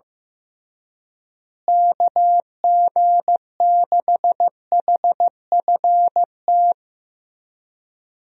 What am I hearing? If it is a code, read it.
Morse code, heard as a keyed tone, about 15 wpm, 700 Hz. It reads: KG6HFT
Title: USA Call Signs - 6 Characters - Repeated 3x - 15wpm